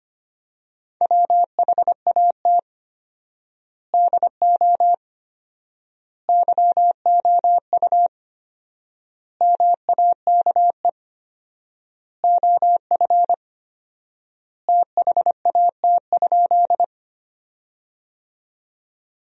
Morse code, heard as a keyed tone, about 25 wpm, 700 Hz. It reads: WHAT DO YOU MAKE OF THAT?
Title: 4x Word-Spacing - Sentences from Top 100 Words - 25wpm